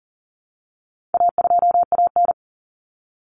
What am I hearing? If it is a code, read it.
Morse code, heard as a keyed tone, about 40 wpm, 700 Hz. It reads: A2AN
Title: Common Cut Numbers - 4-Digit 40wpm